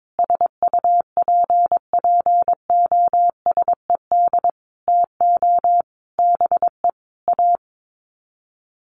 Morse code, heard as a keyed tone, about 22 wpm, 700 Hz. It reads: SUPPOSED TO BE A
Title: Sets of 5 Words Encore 22wpm